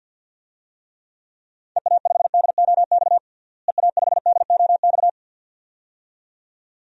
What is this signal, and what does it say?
Morse code, heard as a keyed tone, about 48 wpm, 700 Hz. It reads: EA5DOX EA5DOX
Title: DXCC-100 Call Signs - 6 Characters - Repeated 2x - 50wpm